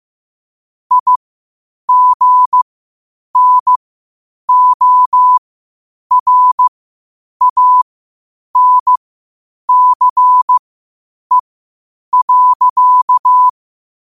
Morse code, heard as a keyed tone, about 15 wpm, spaced as for 8 wpm, 1000 Hz. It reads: IGNORANCE.
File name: qotdfiles / 8wpm